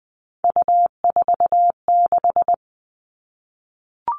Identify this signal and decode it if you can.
Morse code, heard as a keyed tone, about 20 wpm, 700 Hz. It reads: U46
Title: All Cut Numbers - 3-Digit 20wpm